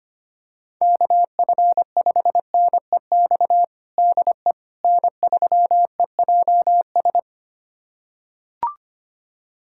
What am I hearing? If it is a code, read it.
Morse code, heard as a keyed tone, about 25 wpm, 700 Hz. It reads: KF5NEX DE N3EJS
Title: USA Callsign DE Callsign 25wpm